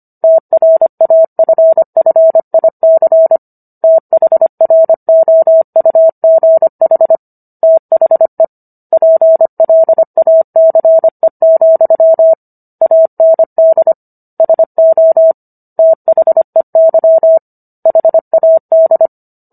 Morse code, heard as a keyed tone, about 25 wpm, 650 Hz. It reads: TRAFFIC THROUGH THE PLACE, AND SO THEY HAD
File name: Book One / 16-Chapter 16 - 25 WPM